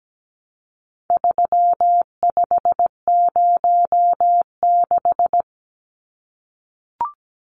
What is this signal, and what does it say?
Morse code, heard as a keyed tone, about 17 wpm, 700 Hz. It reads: 3506
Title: Numbers - 4-Digit 17wpm